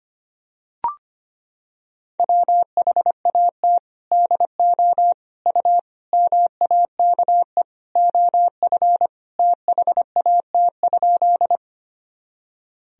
Morse code, heard as a keyed tone, about 25 wpm, 700 Hz. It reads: WHAT DO U MAKE OF THAT?
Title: Sentences from Top 100 Words and Abbreviations 25wpm